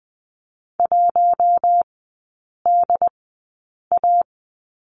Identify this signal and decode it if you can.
Morse code, heard as a keyed tone, about 20 wpm, spaced as for 8 wpm, 700 Hz. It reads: 1DA
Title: 067 Lesson 12e - Letter d - Three Characters 20wpm